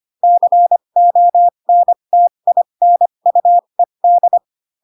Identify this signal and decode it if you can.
Morse code, heard as a keyed tone, about 25 wpm, spaced as for 22 wpm, 700 Hz. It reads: CONTINUED
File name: Pinocchio / Pinocchio ch4 18 25 ik1hge